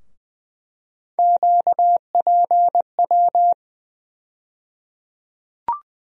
Morse code, heard as a keyed tone, about 21 wpm, 700 Hz. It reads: QPW
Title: USA Call Sign Postfixes - 3 Letters 20wpm